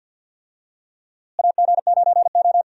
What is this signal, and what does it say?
Morse code, heard as a keyed tone, about 47 wpm, 700 Hz. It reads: AG9O
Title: USA Call Signs - 4 Characters - Repeated 2x - 50wpm